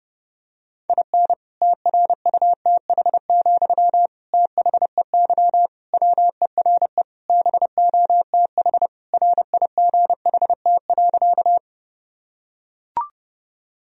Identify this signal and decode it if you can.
Morse code, heard as a keyed tone, about 30 wpm, 700 Hz. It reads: IN TRUTH, THEY WERE BOTH RIGHT.
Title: Sentences from Top 1000 Words - Review Part 1 - 30wpm